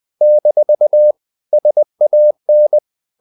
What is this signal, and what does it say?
Morse code, heard as a keyed tone, about 20 wpm, 600 Hz. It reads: - SAN
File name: CW / Practice / NationalCapitals20